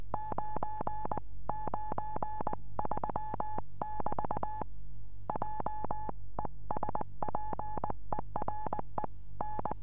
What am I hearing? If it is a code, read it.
Morse code, heard as a keyed tone, about 21 wpm, 850 Hz. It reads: 993 <BT> J ESPERE N
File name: 432 eme / f6eti-dl9kr 432